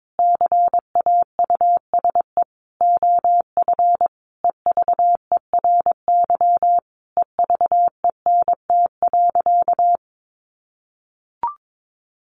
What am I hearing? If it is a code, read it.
Morse code, heard as a keyed tone, about 22 wpm, 700 Hz. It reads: CAUSE OF EVERY EVENT.
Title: Sentences from Top 1400 Words 22wpm